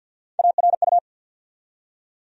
Morse code, heard as a keyed tone, about 53 wpm, 700 Hz. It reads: ARU
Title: USA Call Sign Postfixes - 3 Letters 50wpm